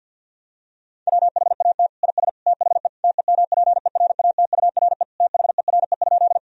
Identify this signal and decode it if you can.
Morse code, heard as a keyed tone, about 50 wpm, 700 Hz. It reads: WHAT IS THE TEMPERATURE THERE?
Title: Sentences from Top 1000 Words - Review Part 4 - 50wpm